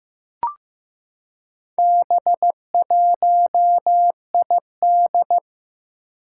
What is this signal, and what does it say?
Morse code, heard as a keyed tone, about 15 wpm, 700 Hz. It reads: B1ID
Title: Four Challenging Letter-Number Combinations 15wpm